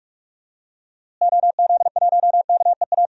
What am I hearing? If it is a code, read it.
Morse code, heard as a keyed tone, about 47 wpm, 700 Hz. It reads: OZ1KEA
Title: DXCC-100 Call Signs - 6 Characters - Repeated 2x - 45wpm